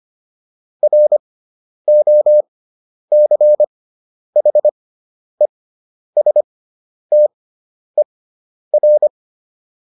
Morse code, heard as a keyed tone, about 25 wpm, spaced as for 10 wpm, 600 Hz. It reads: ROCHESTER